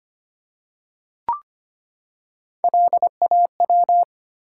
Morse code, heard as a keyed tone, about 25 wpm, 700 Hz. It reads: LAW
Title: Three Letter Words - 25wpm